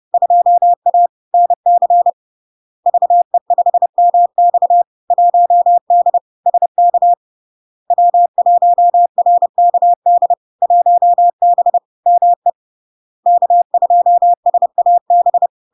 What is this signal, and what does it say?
Morse code, heard as a keyed tone, about 30 wpm, 700 Hz. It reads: 2A NC VE5MX 1D SK W1RKD 1B ME K2SAB
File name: LARK-FD-2022-Log-in-CW-30wpm-700-Hz